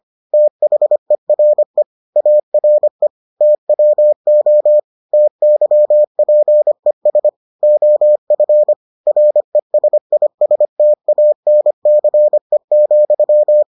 Morse code, heard as a keyed tone, about 25 wpm, 600 Hz. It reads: THERE ARE TWO TYPES OF RESISTANCE,